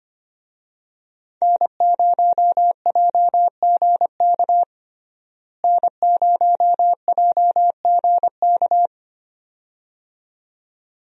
Morse code, heard as a keyed tone, about 25 wpm, 700 Hz. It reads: N0JGK N0JGK
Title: USA Call Signs - 5 Characters - Repeated 2x - 25wpm